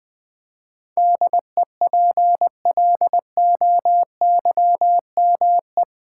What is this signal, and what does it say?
Morse code, heard as a keyed tone, about 20 wpm, 700 Hz. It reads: DEPLOYME
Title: Build A Word - 4 Letters At A Time - 10-Letter Words - 20wpm